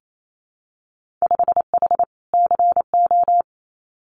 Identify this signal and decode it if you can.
Morse code, heard as a keyed tone, about 28 wpm, 700 Hz. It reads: 5H CO
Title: Winter Field Day Exchanges Encore 28wpm